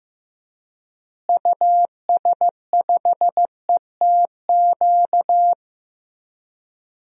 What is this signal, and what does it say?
Morse code, heard as a keyed tone, about 15 wpm, 700 Hz. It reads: US5ETQ
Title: DXCC-100 Call Signs - 6 Characters - Repeated 3x - 15wpm